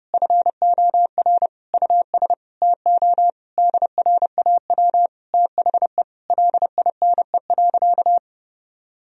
Morse code, heard as a keyed tone, about 30 wpm, 700 Hz. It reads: FOR US TO DRAW THE LINE.
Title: Sentences from Top 1000 Words - Review Part 1 - 30wpm